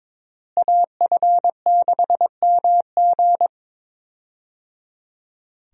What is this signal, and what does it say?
Morse code, heard as a keyed tone, about 22 wpm, 700 Hz. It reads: AF6MG
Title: USA Call Signs - 5 Characters - Repeated 3x - 22wpm